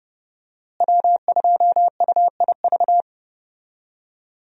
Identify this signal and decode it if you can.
Morse code, heard as a keyed tone, about 30 wpm, 700 Hz. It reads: W2UIV
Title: USA Call Signs - 5 Characters - Repeated 3x - 30wpm